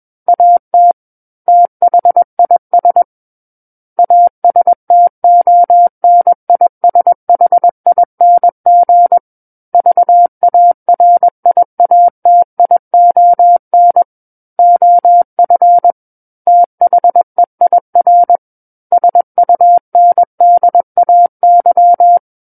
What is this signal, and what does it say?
Morse code, heard as a keyed tone, about 21 wpm, 700 Hz. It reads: AT THIS ASTONISHING VARIATION OF THEIR SUNDAY
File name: Book One / 12-Chapter 12 - 21 WPM